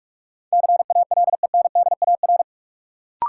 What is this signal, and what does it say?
Morse code, heard as a keyed tone, about 46 wpm, 700 Hz. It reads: CALENDAR
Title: Sentences from Top 1600 Words - New Word Drill - 45wpm